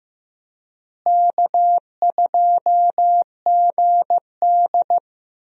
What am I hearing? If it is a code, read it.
Morse code, heard as a keyed tone, about 15 wpm, 700 Hz. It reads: K2GD
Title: USA Call Signs - 4 Characters - Repeated 2x - 15wpm